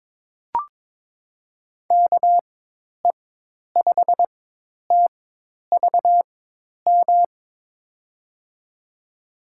Morse code, heard as a keyed tone, about 22 wpm, spaced as for 10 wpm, 700 Hz. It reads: KE5TVM
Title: USA Call Signs - 4x Character Spacing - 22wpm